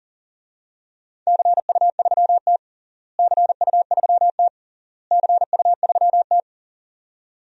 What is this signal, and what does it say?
Morse code, heard as a keyed tone, about 40 wpm, 700 Hz. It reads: CU3T CU3T CU3T
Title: DXCC-100 Call Signs - 4 Characters - Repeated 3x - 40wpm